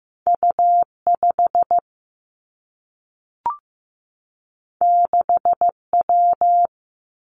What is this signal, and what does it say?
Morse code, heard as a keyed tone, about 15 wpm, 700 Hz. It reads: U 5 6 W
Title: Two Challenging Letter-Number Combinations 15wpm